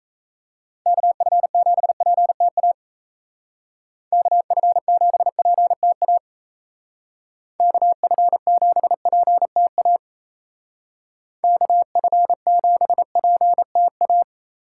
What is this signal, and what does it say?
Morse code, heard as a keyed tone, about 34 wpm, 700 Hz. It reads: KF7PTA KF7PTA KF7PTA KF7PTA
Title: Speed Racing - USA 2x3 Call Signs 28wpm